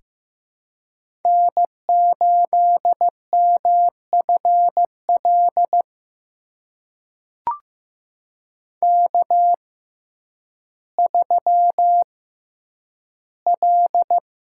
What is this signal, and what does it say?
Morse code, heard as a keyed tone, about 15 wpm, spaced as for 9 wpm, 700 Hz. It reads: N8MFL K3L
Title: USA Call Signs - 5 Characters - 6x Character Spacing - 15wpm